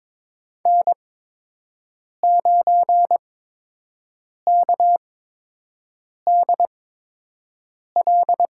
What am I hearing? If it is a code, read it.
Morse code, heard as a keyed tone, about 22 wpm, spaced as for 6 wpm, 700 Hz. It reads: N9KDL
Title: USA Call Signs - 8x Character Spacing - 22wpm